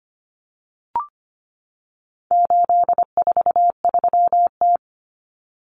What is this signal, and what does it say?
Morse code, heard as a keyed tone, about 25 wpm, 700 Hz. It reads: 843T
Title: Common Cut Numbers - 4-Digit 25wpm